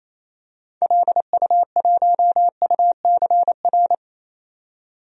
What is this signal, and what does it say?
Morse code, heard as a keyed tone, about 28 wpm, 700 Hz. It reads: LU1UCR
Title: DXCC-100 Call Signs - 6 Characters - 28wpm